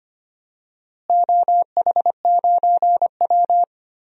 Morse code, heard as a keyed tone, about 25 wpm, 700 Hz. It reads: OH9W
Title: DXCC-100 Call Signs - 4 Characters - Repeated 3x - 25wpm